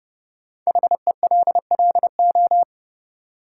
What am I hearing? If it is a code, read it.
Morse code, heard as a keyed tone, about 32 wpm, 700 Hz. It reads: HELLO
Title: Sentences from Top 1100 Words - New Word Drill Alt - 30wpm